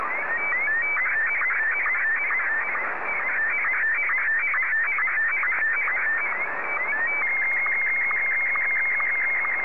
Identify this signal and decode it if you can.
SSTV (slow-scan television), Martin 1.
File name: sstv / M1-JA3QJQ